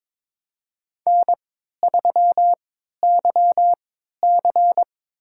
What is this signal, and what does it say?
Morse code, heard as a keyed tone, about 21 wpm, spaced as for 12 wpm, 700 Hz. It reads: N3YC
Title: USA Call Signs - 4 Characters - 3x Character Spacing - 22wpm